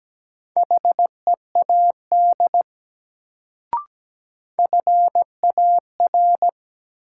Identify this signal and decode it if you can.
Morse code, heard as a keyed tone, about 17 wpm, 700 Hz. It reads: HEAD FAR
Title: Sets of 3 Words - New Word Drill - 17wpm